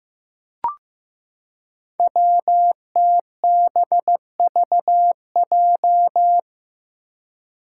Morse code, heard as a keyed tone, about 15 wpm, 700 Hz. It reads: WTBVJ
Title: Five Challenging Letter-Number Combinations 15wpm